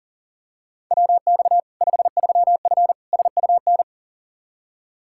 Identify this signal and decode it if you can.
Morse code, heard as a keyed tone, about 40 wpm, 700 Hz. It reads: WX 53F SUN
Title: Practice QSO - Exchange 2 - Info - 40wpm